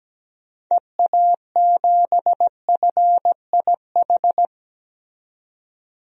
Morse code, heard as a keyed tone, about 17 wpm, 700 Hz. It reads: EA7FIH
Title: DXCC-100 Call Signs - 6 Characters - 17wpm